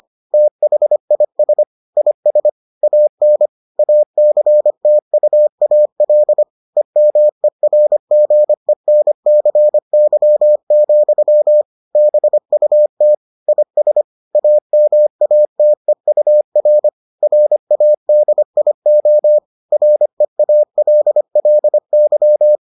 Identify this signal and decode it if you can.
Morse code, heard as a keyed tone, about 25 wpm, 600 Hz. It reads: THIS IS AN ACTUAL EMERGENCY, BUT IS AMATEUR RADIO REALLY